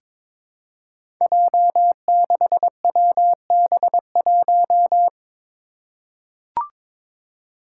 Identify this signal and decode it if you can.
Morse code, heard as a keyed tone, about 22 wpm, 700 Hz. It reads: J6WB1
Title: Five Challenging Letter-Number Combinations 22wpm